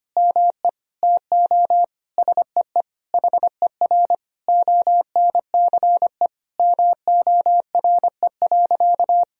Morse code, heard as a keyed tone, about 25 wpm, 700 Hz. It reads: ME TO SEE HER ONCE MORE.
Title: Sentences from Top 500 Words and Abbreviations 25wpm